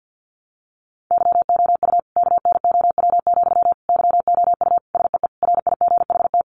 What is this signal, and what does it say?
Morse code, heard as a keyed tone, about 48 wpm, 700 Hz. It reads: YOU KNOW, YOU SEE RIGHT
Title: Sentences from Top 200 Words 50wpm